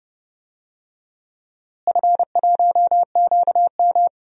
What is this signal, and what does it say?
Morse code, heard as a keyed tone, about 30 wpm, 700 Hz. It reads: F1QM
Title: DXCC-100 Call Signs - 4 Characters - Repeated 3x - 30wpm